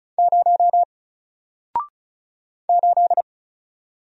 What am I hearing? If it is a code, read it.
Morse code, heard as a keyed tone, about 34 wpm, 700 Hz. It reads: 0 8
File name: Single-Letter-Number-35wpm